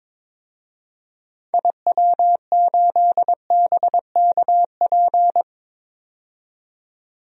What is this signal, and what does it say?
Morse code, heard as a keyed tone, about 22 wpm, 700 Hz. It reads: IW8BKP
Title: DXCC-100 Call Signs - 6 Characters - 22wpm